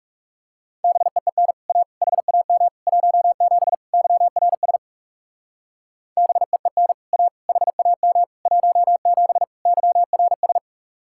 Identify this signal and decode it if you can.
Morse code, heard as a keyed tone, about 37 wpm, 700 Hz. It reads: BEEN A HAM 17 YRS BEEN A HAM 17 YRS
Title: Speed-Racing Common QSO Phrases Encore - 30wpm